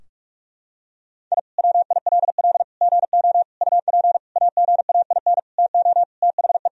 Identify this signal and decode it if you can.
Morse code, heard as a keyed tone, about 46 wpm, 700 Hz. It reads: I WILL GO UP AGAIN TO THE
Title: Sentences from Top 1000 Words - Review Part 3 - 45wpm